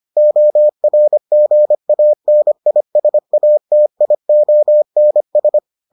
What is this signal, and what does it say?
Morse code, heard as a keyed tone, about 25 wpm, 600 Hz. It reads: ORGANISATIONS